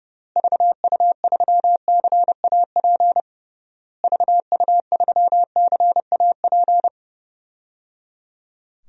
Morse code, heard as a keyed tone, about 30 wpm, 700 Hz. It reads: VU3CAP VU3CAP
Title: DXCC-100 Call Signs - 6 Characters - Repeated 3x - 30wpm